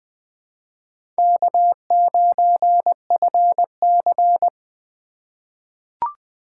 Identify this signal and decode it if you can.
Morse code, heard as a keyed tone, about 20 wpm, 700 Hz. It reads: K9FC
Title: USA Call Signs - 4 Characters - 3x Character Spacing - 20wpm